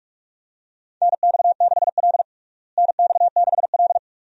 Morse code, heard as a keyed tone, about 46 wpm, 700 Hz. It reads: NX6L NX6L
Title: USA Call Signs - 4 Characters - Repeated 2x - 45wpm